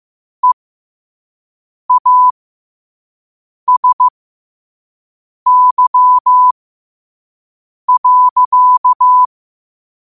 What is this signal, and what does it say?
Morse code, heard as a keyed tone, about 15 wpm, spaced as for 5 wpm, 1000 Hz. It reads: EASY.
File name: qotdfiles / 5wpm